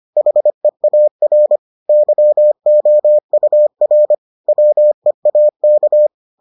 Morse code, heard as a keyed tone, about 25 wpm, 600 Hz. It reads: HEAR YOUR WEAK